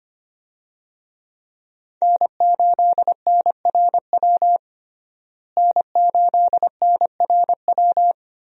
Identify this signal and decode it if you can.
Morse code, heard as a keyed tone, about 25 wpm, 700 Hz. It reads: N8NRW N8NRW
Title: USA Call Signs - 5 Characters - Repeated 2x - 25wpm